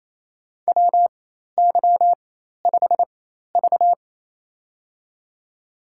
Morse code, heard as a keyed tone, about 28 wpm, spaced as for 13 wpm, 700 Hz. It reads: WY5V
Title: USA Call Signs - 4 Characters - 4x Character Spacing - 28wpm